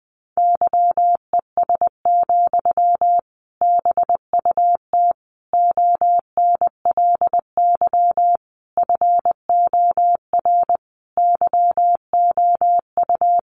Morse code, heard as a keyed tone, about 20 wpm, 700 Hz. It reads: YES, BUT ONLY FOR YOU
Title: Sentences from Top 500 Words 20wpm